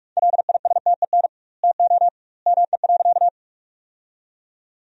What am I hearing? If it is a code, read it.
Morse code, heard as a keyed tone, about 41 wpm, 700 Hz. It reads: LISTEN TO ME.
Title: 2x Word-Spacing - Sentences from Top 400 Words - 45wpm